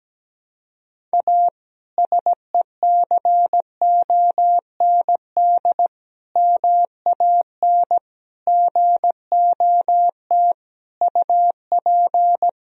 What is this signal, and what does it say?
Morse code, heard as a keyed tone, about 17 wpm, 700 Hz. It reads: A SECOND MAN GOT UP
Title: Sentences from Top 300 Words and Abbreviations 17wpm